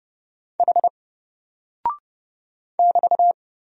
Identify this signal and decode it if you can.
Morse code, heard as a keyed tone, about 30 wpm, 700 Hz. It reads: H <BT>
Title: Single Character (Letter:Number:Punctuation:Prosign) Alt 30wpm